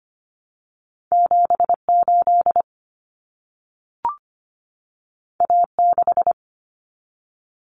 Morse code, heard as a keyed tone, about 25 wpm, 700 Hz. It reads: 7 8 A 6
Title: Common Cut Numbers - 2-Digit 25wpm